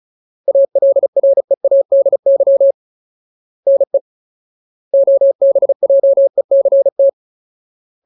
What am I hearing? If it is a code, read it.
Morse code, heard as a keyed tone, about 34 wpm, 550 Hz. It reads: ALREADY NE OBJECT
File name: Ultimate Word List / 35WPM Ultimate